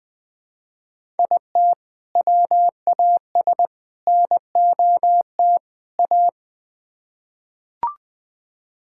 Morse code, heard as a keyed tone, about 20 wpm, 700 Hz. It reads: IT WAS NOT A
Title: Build A Phrase - 4-Word Phrases - 20wpm